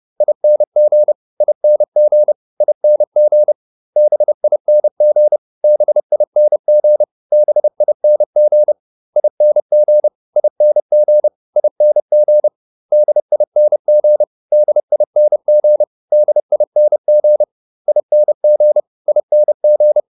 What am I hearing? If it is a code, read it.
Morse code, heard as a keyed tone, about 30 wpm, 600 Hz. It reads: ING ING ING BING BING BING ING ING ING DING DING DING ING ING